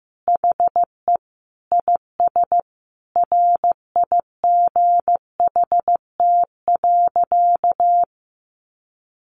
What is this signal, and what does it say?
Morse code, heard as a keyed tone, about 15 wpm, 700 Hz. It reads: HE IS RIGHT.
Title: Sentences from Top 1300 Words 15wpm